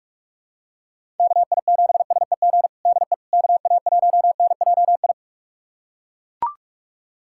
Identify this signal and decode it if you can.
Morse code, heard as a keyed tone, about 42 wpm, 700 Hz. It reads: KI7SEG DE KA1NJI
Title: USA Callsign DE Callsign - 6 Characters - 45wpm